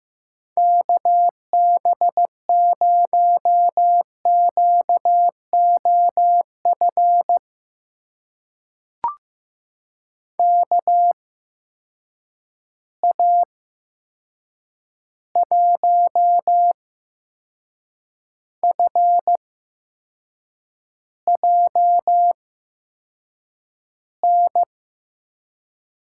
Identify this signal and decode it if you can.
Morse code, heard as a keyed tone, about 15 wpm, spaced as for 6 wpm, 700 Hz. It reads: KB0QOF KA1FJN
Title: USA Call Signs - 6 Characters - 8x Character Spacing - 15wpm